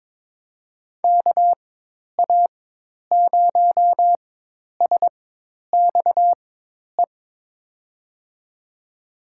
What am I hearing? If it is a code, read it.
Morse code, heard as a keyed tone, about 21 wpm, spaced as for 10 wpm, 700 Hz. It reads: KA0SXE
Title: USA Call Signs - 6 Characters - 4x Character Spacing - 22wpm